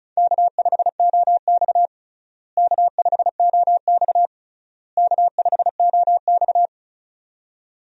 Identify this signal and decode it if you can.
Morse code, heard as a keyed tone, about 34 wpm, 700 Hz. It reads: K5OX K5OX K5OX
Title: USA Call Signs - 4 Characters - Repeated 3x - 35wpm